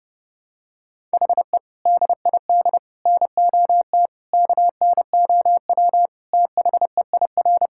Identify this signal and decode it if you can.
Morse code, heard as a keyed tone, about 29 wpm, 700 Hz. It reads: HE DID NOT KNOW THEIR
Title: Sentences from Top 1000 Words - Review Part 6 - 30wpm